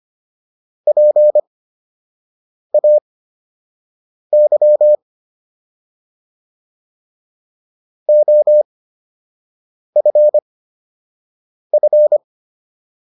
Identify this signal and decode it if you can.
Morse code, heard as a keyed tone, about 25 wpm, spaced as for 6 wpm, 600 Hz. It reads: PAY OFF